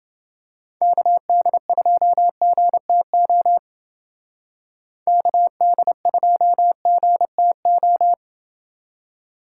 Morse code, heard as a keyed tone, about 28 wpm, 700 Hz. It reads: KD2GTO KD2GTO